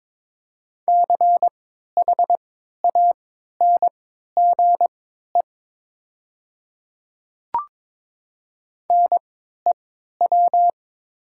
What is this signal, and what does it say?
Morse code, heard as a keyed tone, about 22 wpm, spaced as for 12 wpm, 700 Hz. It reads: CHANGE NEW
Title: 3x Character Spacing - Top 200 Words - 22wpm